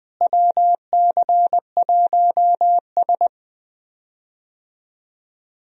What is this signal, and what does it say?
Morse code, heard as a keyed tone, about 20 wpm, 700 Hz. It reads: WC1S
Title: USA Call Signs - 4 Characters - Repeated 3x - 20wpm